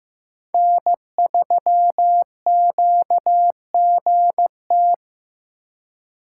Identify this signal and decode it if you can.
Morse code, heard as a keyed tone, about 18 wpm, 700 Hz. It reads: N3QGT
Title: Speed Racing - USA 1x3 Call Signs 15wpm